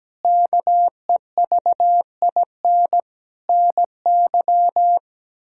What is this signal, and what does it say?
Morse code, heard as a keyed tone, about 17 wpm, 700 Hz. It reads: KEVIN NY
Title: K1USN SST QSO Practice - 17wpm